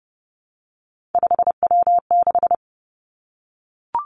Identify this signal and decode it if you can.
Morse code, heard as a keyed tone, about 30 wpm, 700 Hz. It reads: 5W6
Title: Three Challenging Letter-Number Combinations 30wpm